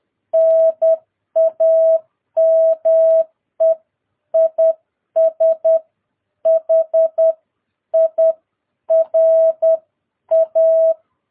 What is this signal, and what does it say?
Morse code, heard as a keyed tone, about 10 wpm, spaced as for 9 wpm, 650 Hz. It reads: NAME IS HIRA
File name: jh8cbh / 2022cw